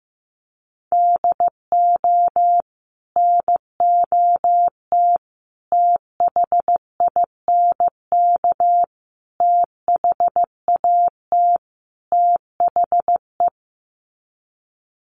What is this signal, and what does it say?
Morse code, heard as a keyed tone, about 15 wpm, 700 Hz. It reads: DO NOT THINK THAT THE
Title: Sets of 5 Words Encore2 15wpm